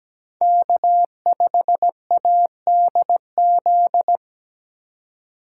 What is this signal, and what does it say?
Morse code, heard as a keyed tone, about 17 wpm, 700 Hz. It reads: K5ADZ
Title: USA Call Signs - 2x Character Spacing - 17wpm